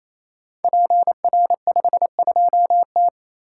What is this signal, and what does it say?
Morse code, heard as a keyed tone, about 29 wpm, 700 Hz. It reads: PR52T
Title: Code Groups - 5 Letter:Numbers - 28wpm